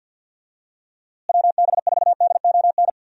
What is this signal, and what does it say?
Morse code, heard as a keyed tone, about 50 wpm, 700 Hz. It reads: WB4DON
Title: USA Call Signs - 6 Characters - Repeated 2x - 50wpm